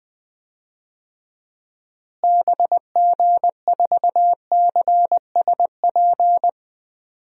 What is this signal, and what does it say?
Morse code, heard as a keyed tone, about 20 wpm, 700 Hz. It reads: BG4CSP
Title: DXCC-100 Call Signs - 6 Characters - Repeated 3x - 20wpm